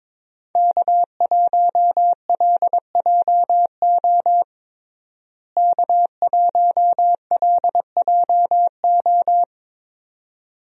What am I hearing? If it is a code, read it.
Morse code, heard as a keyed tone, about 21 wpm, 700 Hz. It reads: K1LJO K1LJO
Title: USA Call Signs - 5 Characters - Repeated 2x - 22wpm